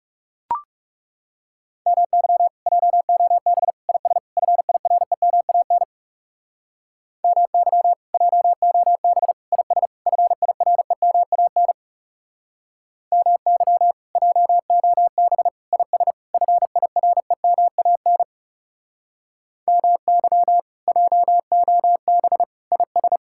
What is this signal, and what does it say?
Morse code, heard as a keyed tone, about 38 wpm, 700 Hz. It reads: MY JOB IS FIREMAN MY JOB IS FIREMAN MY JOB IS FIREMAN MY JOB IS